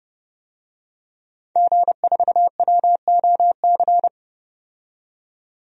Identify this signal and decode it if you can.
Morse code, heard as a keyed tone, about 30 wpm, 700 Hz. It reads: G4WOC
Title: DXCC-100 Call Signs - 5 Characters - 30wpm